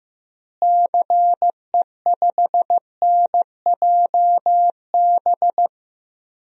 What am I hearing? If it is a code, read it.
Morse code, heard as a keyed tone, about 15 wpm, 700 Hz. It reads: CE5NJB
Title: DXCC-100 Call Signs - 6 Characters - Repeated 3x - 15wpm